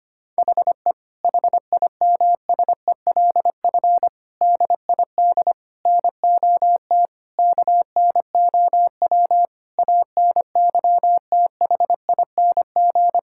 Morse code, heard as a keyed tone, about 25 wpm, 700 Hz. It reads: HE HIMSELF DID NOT KNOW ANYTHING
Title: Sentences from Top 1600 Words 25wpm